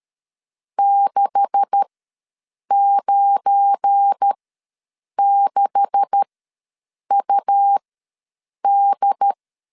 Morse code, heard as a keyed tone, about 13 wpm, spaced as for 7 wpm, 800 Hz. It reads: 696UD
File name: LEZIONI / 9 286 ws 12wpm 3351